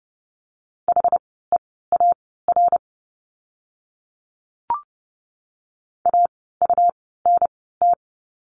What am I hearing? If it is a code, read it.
Morse code, heard as a keyed tone, about 30 wpm, spaced as for 17 wpm, 700 Hz. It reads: HEAR AUNT
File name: Lesson 17f - Letter u - Words 30wpm